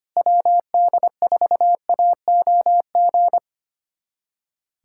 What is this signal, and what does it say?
Morse code, heard as a keyed tone, about 25 wpm, 700 Hz. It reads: WD4AOG
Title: USA Callsign DE Callsign 25wpm